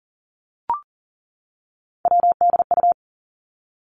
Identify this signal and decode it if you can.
Morse code, heard as a keyed tone, about 41 wpm, 700 Hz. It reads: WDU
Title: Three Challenging Letter-Number Combinations 40wpm